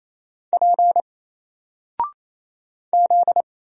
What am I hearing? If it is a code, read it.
Morse code, heard as a keyed tone, about 27 wpm, 700 Hz. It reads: P Z
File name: Single-Letter-Number-28wpm